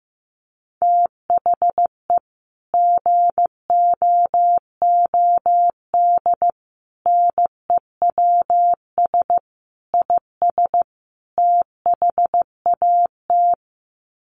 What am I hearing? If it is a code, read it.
Morse code, heard as a keyed tone, about 15 wpm, 700 Hz. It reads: THE GOOD NEWS IS THAT
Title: Sets of 5 Words 15wpm